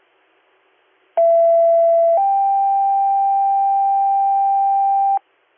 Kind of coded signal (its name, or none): two-tone page